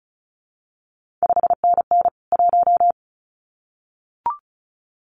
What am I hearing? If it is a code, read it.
Morse code, heard as a keyed tone, about 36 wpm, 700 Hz. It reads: 5NN 1
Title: CQ WW DX Contest Exchange Encore - 35wpm